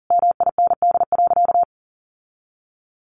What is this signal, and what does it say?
Morse code, heard as a keyed tone, about 40 wpm, 700 Hz. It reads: MIND.
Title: Sentences from Top 1500 Words 40wpm